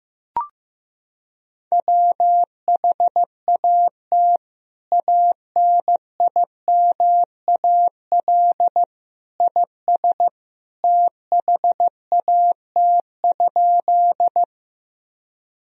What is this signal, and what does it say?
Morse code, heard as a keyed tone, about 15 wpm, 700 Hz. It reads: WHAT ANIMAL IS THAT?
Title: Sentences from Top 1000 Words - Review Part 1 - 15wpm